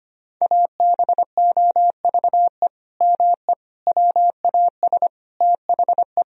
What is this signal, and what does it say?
Morse code, heard as a keyed tone, about 25 wpm, 700 Hz. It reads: ABOVE ME WAS THE
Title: Sentences from Top 1000 Words - Review Part 3 - 25wpm